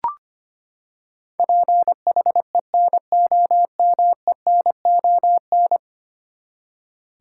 Morse code, heard as a keyed tone, about 25 wpm, 700 Hz. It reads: PHENOMENON
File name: Ten Letter Words - 25wpm